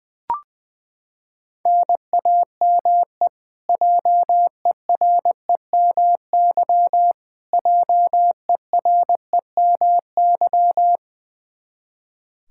Morse code, heard as a keyed tone, about 20 wpm, 700 Hz. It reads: NAME JEREMY JEREMY
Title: Practice QSO - Exchange 1 - Beginning - 20wpm